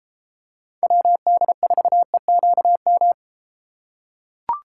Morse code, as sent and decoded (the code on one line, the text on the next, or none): .-- -.. ....- . --.- --
WD4EQM